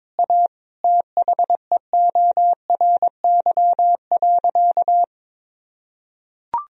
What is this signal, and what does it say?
Morse code, heard as a keyed tone, about 22 wpm, 700 Hz. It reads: A THEORY.
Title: Sentences from Top 1000 Words - Review Part 3 - 22wpm